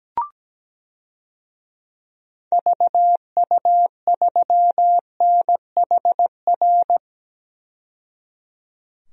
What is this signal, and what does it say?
Morse code, heard as a keyed tone, about 17 wpm, 700 Hz. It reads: VU3NHR
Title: DXCC-100 Call Signs - 6 Characters - 17wpm